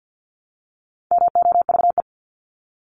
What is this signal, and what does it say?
Morse code, heard as a keyed tone, about 48 wpm, 700 Hz. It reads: MOVE
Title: ING Suffix - Top 100 Words with Base Words 50wpm